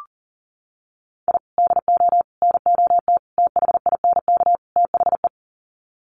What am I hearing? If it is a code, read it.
Morse code, heard as a keyed tone, about 40 wpm, 700 Hz. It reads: I DO NOT THINK THE
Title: Sets of 5 Words 40wpm